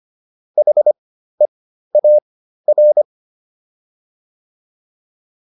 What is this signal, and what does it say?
Morse code, heard as a keyed tone, about 25 wpm, spaced as for 13 wpm, 600 Hz. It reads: HEAR